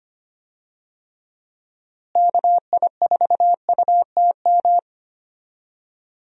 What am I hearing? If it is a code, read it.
Morse code, heard as a keyed tone, about 25 wpm, 700 Hz. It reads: KI4UTM
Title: USA Call Signs - 6 Characters - Repeated 3x - 25wpm